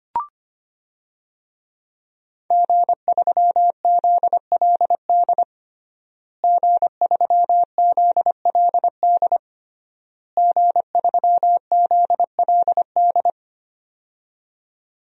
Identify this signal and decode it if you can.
Morse code, heard as a keyed tone, about 25 wpm, 700 Hz. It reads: G3ZLD G3ZLD G3ZLD
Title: DXCC-100 Call Signs - 5 Characters - Repeated 3x - 25wpm